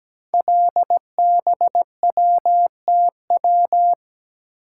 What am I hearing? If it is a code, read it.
Morse code, heard as a keyed tone, about 17 wpm, 700 Hz. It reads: LBWTW
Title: Code Groups - 5 Letter:Numbers - 17wpm